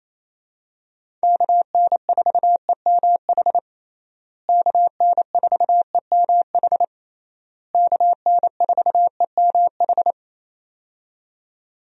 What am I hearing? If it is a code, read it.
Morse code, heard as a keyed tone, about 28 wpm, 700 Hz. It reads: KN4EMH KN4EMH KN4EMH
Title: USA Call Signs - 6 Characters - Repeated 3x - 28wpm